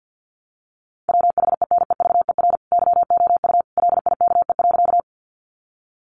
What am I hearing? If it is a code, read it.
Morse code, heard as a keyed tone, about 50 wpm, 700 Hz. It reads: WHENEVER YOU LIKE.
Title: Sentences from Top 1200 Words 50wpm